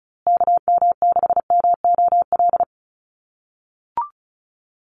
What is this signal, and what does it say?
Morse code, heard as a keyed tone, about 34 wpm, 700 Hz. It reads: KM6MOL
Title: USA Call Signs Repeated 2x 35wpm